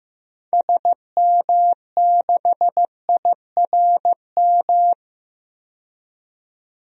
Morse code, heard as a keyed tone, about 15 wpm, 700 Hz. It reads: SM6IRM
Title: DXCC-100 Call Signs - 6 Characters - 15wpm